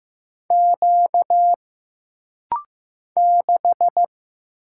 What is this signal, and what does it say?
Morse code, heard as a keyed tone, about 15 wpm, 700 Hz. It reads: Q 6
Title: Single Letter-Number Alt - Rapid-Fire 15wpm